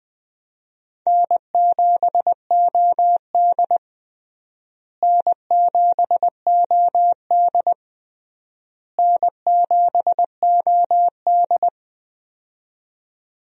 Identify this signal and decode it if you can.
Morse code, heard as a keyed tone, about 21 wpm, 700 Hz. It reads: N7OD N7OD N7OD
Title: USA Call Signs - 4 Characters - Repeated 3x - 20wpm